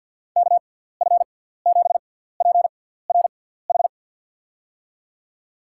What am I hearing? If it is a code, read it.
Morse code, heard as a keyed tone, about 52 wpm, spaced as for 17 wpm, 700 Hz. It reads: KF7PRH
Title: USA Call Signs - 6 Characters - 6x Character Spacing - 50wpm